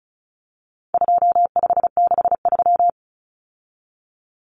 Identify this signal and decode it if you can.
Morse code, heard as a keyed tone, about 34 wpm, 700 Hz. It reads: 2563
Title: Common Cut Numbers - 4-Digit 35wpm